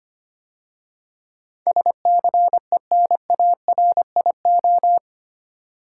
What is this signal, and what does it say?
Morse code, heard as a keyed tone, about 25 wpm, 700 Hz. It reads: SCENARIO
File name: 8 Letter Words - First 3 Letters Given - 25wpm